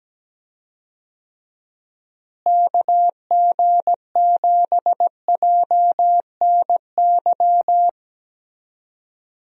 Morse code, heard as a keyed tone, about 17 wpm, 700 Hz. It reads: KG7JNY
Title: USA Call Signs - 6 Characters - Repeated 2x - 17wpm